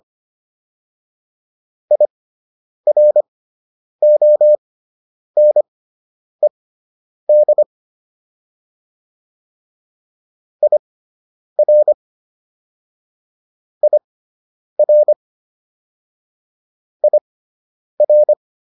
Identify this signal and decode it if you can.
Morse code, heard as a keyed tone, about 25 wpm, spaced as for 9 wpm, 600 Hz. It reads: IRONED IR IR IR